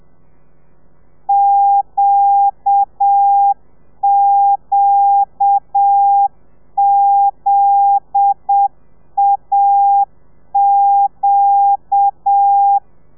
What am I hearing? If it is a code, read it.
Morse code, heard as a keyed tone, about 7 wpm, 800 Hz. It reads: QQZAQ